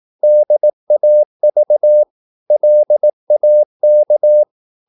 Morse code, heard as a keyed tone, about 18 wpm, 600 Hz. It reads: DAV LAK
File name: Korte ord DK 18wpm